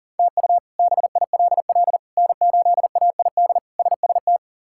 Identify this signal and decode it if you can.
Morse code, heard as a keyed tone, about 40 wpm, 700 Hz. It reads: TU BILL N8AID SST
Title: K1USN SST QSO Practice - 40wpm